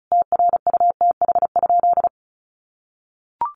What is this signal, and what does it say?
Morse code, heard as a keyed tone, about 34 wpm, 700 Hz. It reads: TRUTH?
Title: Sentences from Top 800 Words 35wpm